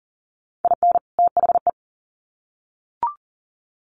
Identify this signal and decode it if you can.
Morse code, heard as a keyed tone, about 40 wpm, 700 Hz. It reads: IN THE
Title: Sets of 2 Words 40wpm